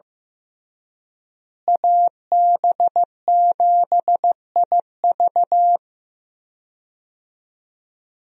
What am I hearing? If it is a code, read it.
Morse code, heard as a keyed tone, about 15 wpm, 700 Hz. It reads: AB7IV
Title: USA Call Signs - 5 Characters - Repeated 3x - 15wpm